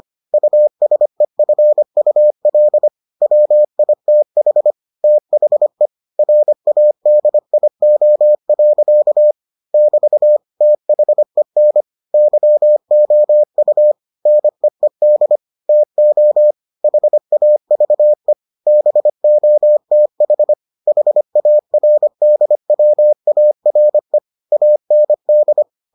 Morse code, heard as a keyed tone, about 25 wpm, 600 Hz. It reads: USEFUL WITH THE RADIO. <BT> THEN YOU NEED TO HAVE BOTH HARDWARE AND